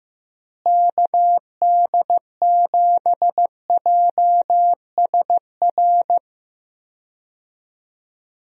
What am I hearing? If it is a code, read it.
Morse code, heard as a keyed tone, about 15 wpm, 700 Hz. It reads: KD7JSR
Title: USA Call Signs - 6 Characters - Repeated 3x - 15wpm